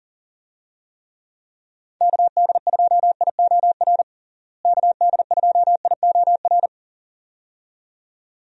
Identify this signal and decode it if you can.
Morse code, heard as a keyed tone, about 40 wpm, 700 Hz. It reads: KD2IOR KD2IOR
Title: USA Call Signs - 6 Characters - Repeated 2x - 40wpm